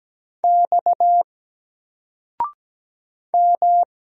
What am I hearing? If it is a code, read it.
Morse code, heard as a keyed tone, about 17 wpm, 700 Hz. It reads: X M
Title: Single Letter-Number 17wpm